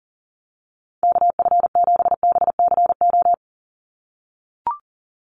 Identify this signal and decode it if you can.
Morse code, heard as a keyed tone, about 40 wpm, 700 Hz. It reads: KF7BCO
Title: USA Call Signs 40wpm